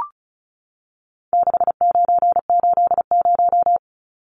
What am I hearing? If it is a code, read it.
Morse code, heard as a keyed tone, about 34 wpm, 700 Hz. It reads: 6980
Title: Numbers - 4-Digit 35wpm